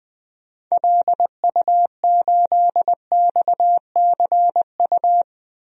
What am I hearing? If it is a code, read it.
Morse code, heard as a keyed tone, about 20 wpm, 700 Hz. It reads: LU8XCU
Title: DXCC-100 Call Signs - 6 Characters - Repeated 2x - 20wpm